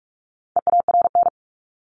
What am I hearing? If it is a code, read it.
Morse code, heard as a keyed tone, about 44 wpm, 700 Hz. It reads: EARN
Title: Top 1251-1500 Words 45wpm